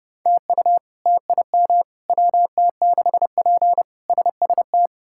Morse code, heard as a keyed tone, about 30 wpm, 700 Hz. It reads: TU TIM WT6P SST
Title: K1USN SST QSO Practice - 30wpm